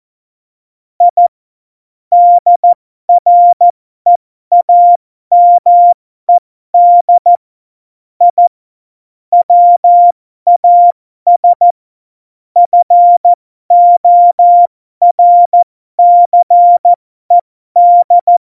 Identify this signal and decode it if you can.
Morse code, heard as a keyed tone, about 14 wpm, spaced as for 12 wpm, 700 Hz. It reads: I DREAMED I WAS FORCED
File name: Morsekurs / Witze